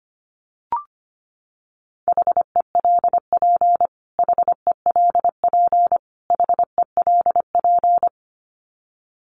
Morse code, heard as a keyed tone, about 25 wpm, 700 Hz. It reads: HELP HELP HELP
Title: Top 300 Words Repeated 25wpm